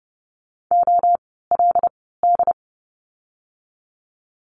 Morse code, heard as a keyed tone, about 30 wpm, spaced as for 17 wpm, 700 Hz. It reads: OLD
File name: course / 30:10wpm / 115 Lesson 19f - Letter w - Words 30wpm